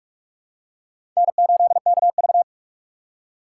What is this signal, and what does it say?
Morse code, heard as a keyed tone, about 46 wpm, 700 Hz. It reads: N8KV
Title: USA Call Signs - 4 Characters - 4x Character Spacing - 45wpm